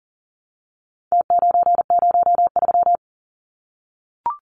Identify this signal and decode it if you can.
Morse code, heard as a keyed tone, about 40 wpm, 700 Hz. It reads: T903
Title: Numbers - Contest Style - 4-Digit 40wpm